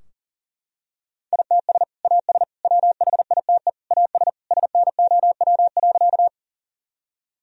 Morse code, heard as a keyed tone, about 40 wpm, 700 Hz. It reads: ITS AS WHITE AS SNOW.
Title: Sentences from Top 1000 Words - Review Part 3 - 40wpm